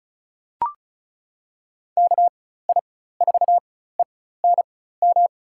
Morse code, heard as a keyed tone, about 36 wpm, spaced as for 16 wpm, 700 Hz. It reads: KI4ENM
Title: USA Call Signs - 6 Characters - 4x Character Spacing - 35wpm